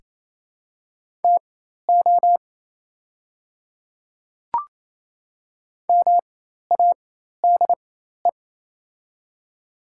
Morse code, heard as a keyed tone, about 28 wpm, spaced as for 13 wpm, 700 Hz. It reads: TO MADE
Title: 4x Character Spacing - Top 200 Words - 28wpm